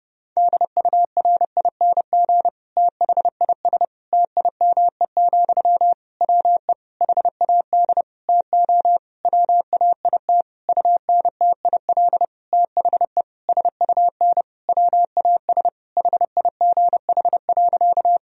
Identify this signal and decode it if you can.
Morse code, heard as a keyed tone, about 31 wpm, 700 Hz. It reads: DURING THIS TIME, WE HAD TO WAIT UNTIL THE SUN WAS HIGH.
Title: Sentences from Top 1000 Words - Review Part 2 - 30wpm